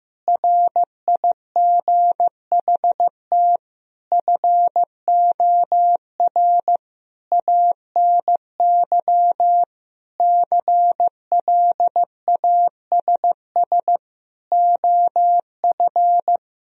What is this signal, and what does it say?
Morse code, heard as a keyed tone, about 16 wpm, 700 Hz. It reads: RIGHT FOR ANY CLASS OF
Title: Sentences from Top 1000 Words - Review Part 2 - 15wpm